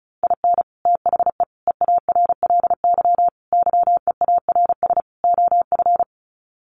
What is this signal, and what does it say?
Morse code, heard as a keyed tone, about 34 wpm, 700 Hz. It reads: IN THE EARLY YEARS OF
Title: Sets of 5 Words Encore 35wpm